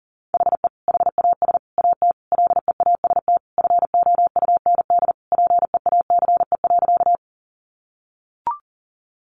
Morse code, heard as a keyed tone, about 40 wpm, 700 Hz. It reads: HE HAS AT LEAST FOUND PEACE.
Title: Sentences from Top 1500 Words 40wpm